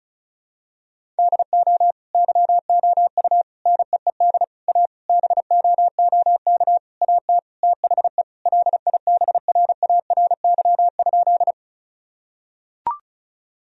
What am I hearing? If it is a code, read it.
Morse code, heard as a keyed tone, about 34 wpm, 700 Hz. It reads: DO YOU NEED A BOOK AT THE LIBRARY?
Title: Sentences from Top 1000 Words - Review Part 4 - 35wpm